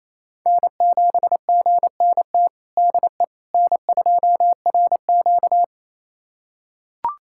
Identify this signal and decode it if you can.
Morse code, heard as a keyed tone, about 28 wpm, 700 Hz. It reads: N7GNT DE N2RQ
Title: USA Callsign DE Callsign 28wpm